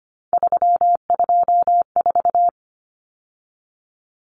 Morse code, heard as a keyed tone, about 25 wpm, 700 Hz. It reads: 324
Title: Numbers - 3-Digit 25wpm